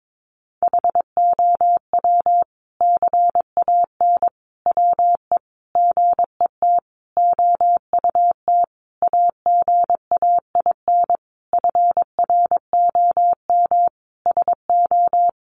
Morse code, heard as a keyed tone, about 22 wpm, 700 Hz. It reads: HOW CAN WE GET OUT AGAIN FROM SO
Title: Sentences from Top 1200 Words 22wpm